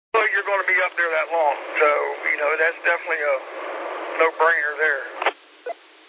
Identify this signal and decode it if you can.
Emergency Alert System (SAME) tones (header bursts), heard as harsh buzzing bursts, with speech. It is heard from 0.6 to 5.1 s.
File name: SimplexLogic